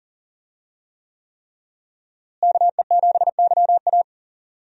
Morse code, heard as a keyed tone, about 40 wpm, 700 Hz. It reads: KE7YA
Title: USA Call Signs - 5 Characters - Repeated 3x - 40wpm